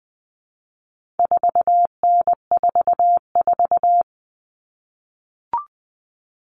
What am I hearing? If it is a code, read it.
Morse code, heard as a keyed tone, about 20 wpm, 700 Hz. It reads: 4N44
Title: All Cut Numbers - 4-Digit 20wpm